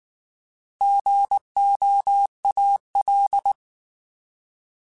Morse code, heard as a keyed tone, about 19 wpm, 800 Hz. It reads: GOAL